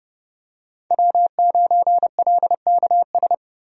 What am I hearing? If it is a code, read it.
Morse code, heard as a keyed tone, about 30 wpm, 700 Hz. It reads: W9LKS
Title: USA Call Signs - 5 Characters - Repeated 2x - 30wpm